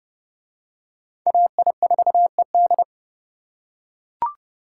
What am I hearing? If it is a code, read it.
Morse code, heard as a keyed tone, about 28 wpm, 700 Hz. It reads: AI4ED